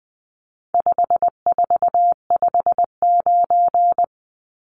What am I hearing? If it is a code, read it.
Morse code, heard as a keyed tone, about 20 wpm, 700 Hz. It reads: 5459
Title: Numbers - 4-Digit 20wpm